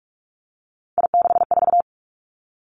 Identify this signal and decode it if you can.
Morse code, heard as a keyed tone, about 46 wpm, 700 Hz. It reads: I64
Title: Three Challenging Letter-Number Combinations 45wpm